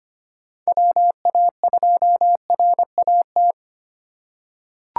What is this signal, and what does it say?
Morse code, heard as a keyed tone, about 25 wpm, 700 Hz. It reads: WA2RAT
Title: USA Call Signs - 2x Character Spacing - 25wpm